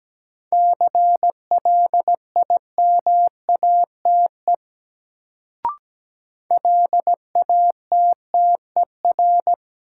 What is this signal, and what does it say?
Morse code, heard as a keyed tone, about 17 wpm, 700 Hz. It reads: CLIMATE LATTER
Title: Sentences from Top 1600 Words - New Word Drill Alt - 17wpm